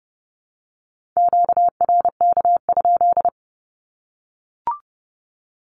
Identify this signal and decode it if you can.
Morse code, heard as a keyed tone, about 30 wpm, 700 Hz. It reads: QRK?
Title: Q-Codes 30wpm